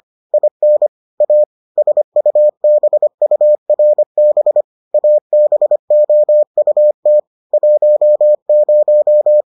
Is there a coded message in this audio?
Yes: Morse code from 0.3 to 9.4 s, heard as a keyed tone, about 25 wpm, 600 Hz. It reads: IN A SUBURB ABOUT 10